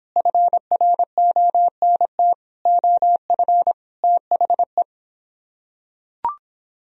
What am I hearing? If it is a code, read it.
Morse code, heard as a keyed tone, about 26 wpm, 700 Hz. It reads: FRONT OF THE